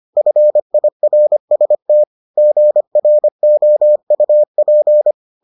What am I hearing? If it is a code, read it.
Morse code, heard as a keyed tone, about 25 wpm, 600 Hz. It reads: FIRST GROUP